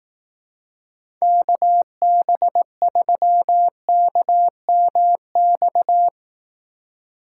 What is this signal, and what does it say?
Morse code, heard as a keyed tone, about 18 wpm, 700 Hz. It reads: KB3KMX
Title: Speed Racing - USA 2x3 Call Signs 15wpm